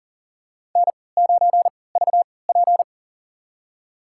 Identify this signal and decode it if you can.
Morse code, heard as a keyed tone, about 37 wpm, spaced as for 22 wpm, 700 Hz. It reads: N9VP
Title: USA Call Signs - 3x Character Spacing - 40wpm